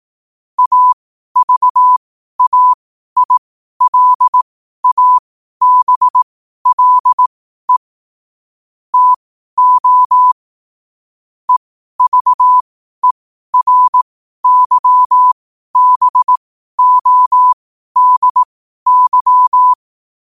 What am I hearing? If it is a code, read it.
Morse code, heard as a keyed tone, about 18 wpm, spaced as for 12 wpm, 1000 Hz. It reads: AVAILABLE TO EVERYBODY
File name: qotdfiles / 12wpm